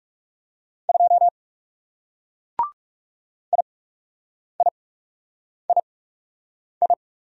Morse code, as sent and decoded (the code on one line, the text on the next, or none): ..--- / .. .. .. ..
2 IIII